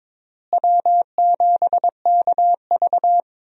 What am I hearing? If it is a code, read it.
Morse code, heard as a keyed tone, about 22 wpm, 700 Hz. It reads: W7KV
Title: USA Call Signs - 4 Characters - Repeated 2x - 22wpm